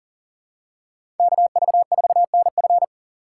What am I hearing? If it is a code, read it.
Morse code, heard as a keyed tone, about 39 wpm, 700 Hz. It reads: KV4NF
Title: USA Call Signs - 6x Character Spacing - 40wpm